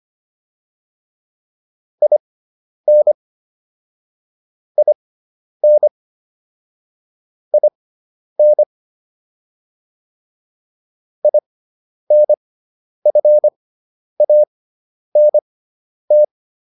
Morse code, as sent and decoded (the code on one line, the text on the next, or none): .. -. / .. -. / .. -. / .. -. ..-. .- -. -
IN IN IN INFANT